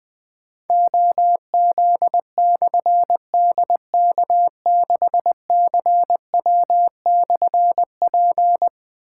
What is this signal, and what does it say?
Morse code, heard as a keyed tone, about 20 wpm, 700 Hz. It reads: OZ/DK6CW/P
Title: Difficult Call Signs 20wpm